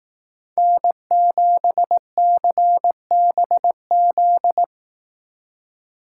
Morse code, heard as a keyed tone, about 18 wpm, 700 Hz. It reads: N7CBZ
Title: Speed Racing - USA Call Signs 15wpm